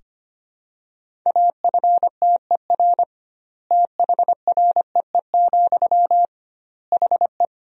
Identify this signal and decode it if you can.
Morse code, heard as a keyed tone, about 25 wpm, 700 Hz. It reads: AFTER THREE, HE
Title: 2x Word-Spacing - Sentences from Top 200 Words - 25wpm